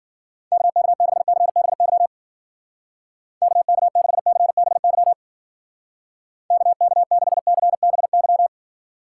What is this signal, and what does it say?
Morse code, heard as a keyed tone, about 50 wpm, 700 Hz. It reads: KK6CBY KK6CBY KK6CBY
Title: Speed Racing - USA 2x3 Call Signs 40wpm